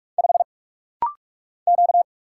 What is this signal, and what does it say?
Morse code, heard as a keyed tone, about 44 wpm, 700 Hz. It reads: 5 Q
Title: Single Letter-Number - Rapid-Fire 45wpm